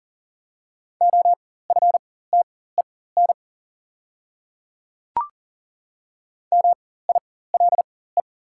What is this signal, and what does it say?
Morse code, heard as a keyed tone, about 40 wpm, spaced as for 19 wpm, 700 Hz. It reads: OFTEN MILE
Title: 4x Character Spacing - Top 300 Words - 40wpm